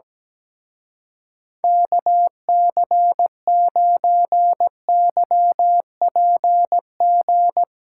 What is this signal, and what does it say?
Morse code, heard as a keyed tone, about 17 wpm, 700 Hz. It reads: KC9YPG
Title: Speed Racing - USA 2x3 Call Signs 17wpm